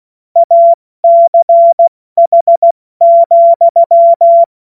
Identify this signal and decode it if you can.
Morse code, heard as a keyed tone, about 16 wpm, 680 Hz. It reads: ACH,
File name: Morsekurs / Witze / Witz 24. August 2021, 70 F80